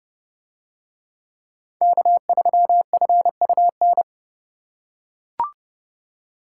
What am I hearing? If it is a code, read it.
Morse code, heard as a keyed tone, about 29 wpm, 700 Hz. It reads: K3FUN